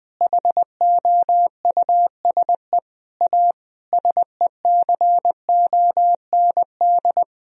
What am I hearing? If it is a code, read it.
Morse code, heard as a keyed tone, about 20 wpm, 700 Hz. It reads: HOUSE A SECOND
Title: Sentences from Top 1000 Words - Review Part 1 - 20wpm